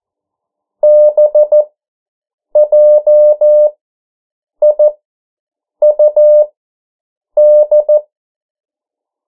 Morse code, heard as a keyed tone, about 14 wpm, spaced as for 7 wpm, 600 Hz. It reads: BJIUD
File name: morse mp3 / 09 urok 8